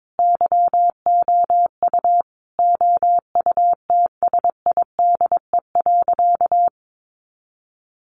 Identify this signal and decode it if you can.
Morse code, heard as a keyed tone, about 22 wpm, 700 Hz. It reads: YOU OUTSIDE.
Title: Sentences from Top 900 Words 22wpm